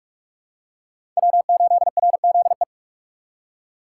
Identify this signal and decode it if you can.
Morse code, heard as a keyed tone, about 42 wpm, 700 Hz. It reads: W8RZE
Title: USA Call Signs - 5 Characters - 3x Character Spacing - 45wpm